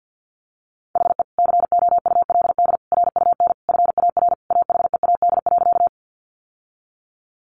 Morse code, heard as a keyed tone, about 50 wpm, 700 Hz. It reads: HE COULD RUN FAR AHEAD.
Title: Sentences from Top 1100 Words 50wpm